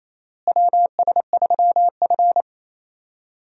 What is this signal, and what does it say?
Morse code, heard as a keyed tone, about 28 wpm, 700 Hz. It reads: WS3F
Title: USA Call Signs - 4 Characters - Repeated 2x - 28wpm